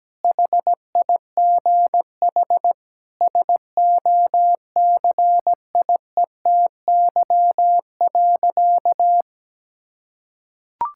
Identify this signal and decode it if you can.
Morse code, heard as a keyed tone, about 17 wpm, 700 Hz. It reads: HIGH SOCIETY.
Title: Sentences from Top 1000 Words - Review Part 6 - 17wpm